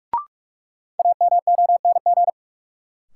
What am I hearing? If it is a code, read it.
Morse code, heard as a keyed tone, about 45 wpm, 700 Hz. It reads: AMONG
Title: Sentences from Top 500 Words - New Word Drill - 45wpm